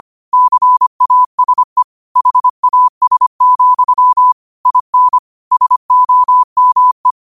Morse code, heard as a keyed tone, about 25 wpm, 1000 Hz. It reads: CASE HAS, IN SOME
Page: Quotes for Wed, 13 Aug 2025 in Morse Code at 25 words per minute.